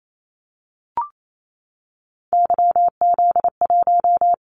Morse code, heard as a keyed tone, about 28 wpm, 700 Hz. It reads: YZ1
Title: Callsign Prefixes for Top 25 DXCC - 28wpm